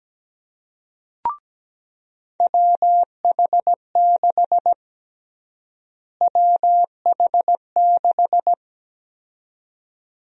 Morse code, heard as a keyed tone, about 17 wpm, 700 Hz. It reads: WH6 WH6
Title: USA Call Sign Prefixes - Repeated 2x - 17wpm